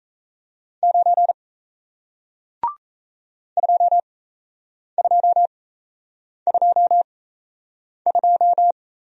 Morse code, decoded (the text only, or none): 9 2222